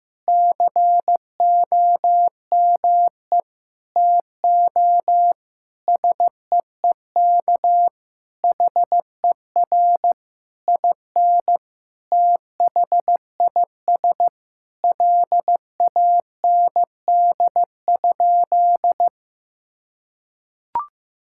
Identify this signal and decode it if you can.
Morse code, heard as a keyed tone, about 15 wpm, 700 Hz. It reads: COME TO SEEK HER IN THIS LAND?
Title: Sentences from Top 1600 Words 15wpm